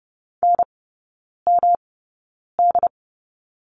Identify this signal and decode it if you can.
Morse code, heard as a keyed tone, about 30 wpm, spaced as for 9 wpm, 700 Hz. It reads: NMD
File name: Lesson 18e - Letter m - Three Characters 30wpm